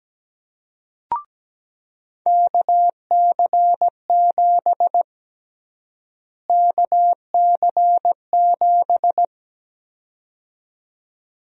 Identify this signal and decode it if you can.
Morse code, heard as a keyed tone, about 17 wpm, 700 Hz. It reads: KC7 KC7
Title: USA Call Sign Prefixes - Repeated 2x - 17wpm